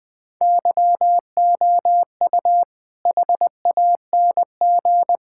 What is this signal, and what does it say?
Morse code, heard as a keyed tone, about 20 wpm, 700 Hz. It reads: YOU HANG
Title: All-Age Appropriate Jokes - 20wpm